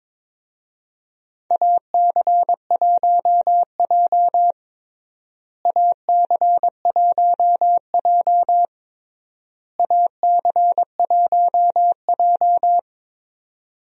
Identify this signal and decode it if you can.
Morse code, heard as a keyed tone, about 22 wpm, 700 Hz. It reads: AC1J AC1J AC1J
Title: USA Call Signs - 4 Characters - Repeated 3x - 22wpm